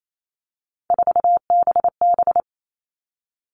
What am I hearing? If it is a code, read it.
Morse code, heard as a keyed tone, about 28 wpm, 700 Hz. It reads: 4BB
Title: All Cut Numbers - 3-Digit 28wpm